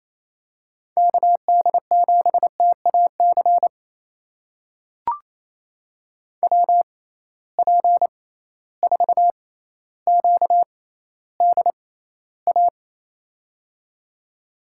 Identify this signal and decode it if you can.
Morse code, heard as a keyed tone, about 28 wpm, spaced as for 14 wpm, 700 Hz. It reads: KD7TAC WP4QDA
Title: USA Call Signs - 6 Characters - 6x Character Spacing - 28wpm